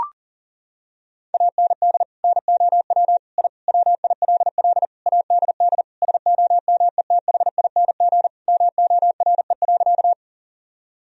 Morse code, heard as a keyed tone, about 40 wpm, 700 Hz. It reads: AND NOW I WILL ADD SOMETHING MORE.
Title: Sentences from Top 1000 Words - Review Part 1 - 40wpm